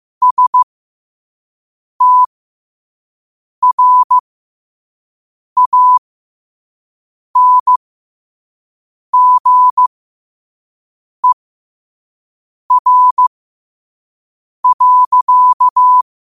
Morse code, heard as a keyed tone, about 15 wpm, spaced as for 5 wpm, 1000 Hz. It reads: STRANGER.
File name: qotdfiles / 5wpm